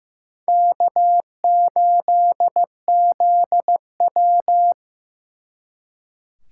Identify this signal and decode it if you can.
Morse code, heard as a keyed tone, about 17 wpm, 700 Hz. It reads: K8ZW
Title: Speed Racing - USA 1x2 Call Signs 15wpm